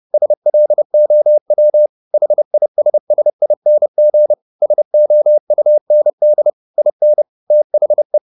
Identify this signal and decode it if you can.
Morse code, heard as a keyed tone, about 30 wpm, 600 Hz. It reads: SLOW HISSING SOUND IN THE
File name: the old man and the sea / the old man and the sea 1 30wpm